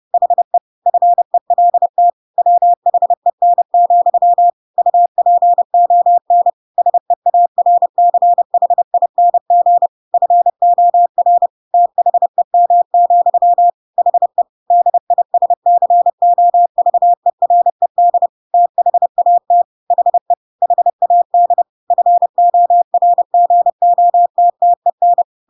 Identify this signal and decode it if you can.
Morse code, heard as a keyed tone, about 30 wpm, 700 Hz. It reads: HE FELT WHEN, UPON SEARCHING FOR THEM, HE DISCOVERED THAT HE HAD FORGOTTEN
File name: Pinocchio / Pinocchio ch3 30 30 ik1hge